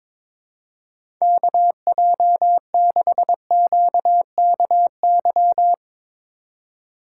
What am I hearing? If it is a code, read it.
Morse code, heard as a keyed tone, about 22 wpm, 700 Hz. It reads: KJ6QKY
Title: USA Call Signs - 6 Characters - 2x Character Spacing - 22wpm